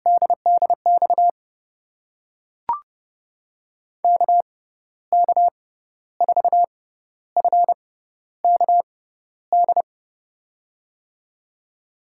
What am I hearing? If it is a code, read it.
Morse code, heard as a keyed tone, about 32 wpm, spaced as for 13 wpm, 700 Hz. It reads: DDX KK4FKD
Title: USA Call Signs - 6 Characters - 6x Character Spacing - 30wpm